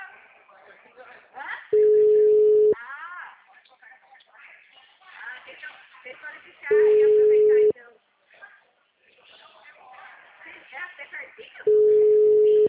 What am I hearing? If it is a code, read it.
Ringback tone (Europe, 425 Hz), with speech.
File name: out- -2005- - - .270